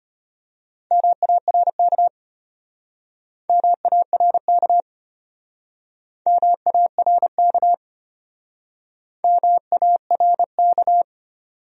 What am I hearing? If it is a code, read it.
Morse code, heard as a keyed tone, about 35 wpm, 700 Hz. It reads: MARK MARK MARK MARK